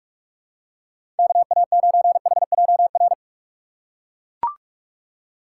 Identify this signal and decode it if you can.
Morse code, heard as a keyed tone, about 42 wpm, 700 Hz. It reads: KA9HJR
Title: USA Call Signs - 6 Characters - 6x Character Spacing - 45wpm